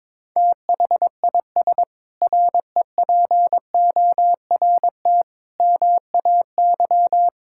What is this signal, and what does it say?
Morse code, heard as a keyed tone, about 22 wpm, 700 Hz. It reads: THIS REPORT MAY
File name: Sentences with 7-Word Phrases - 22wpm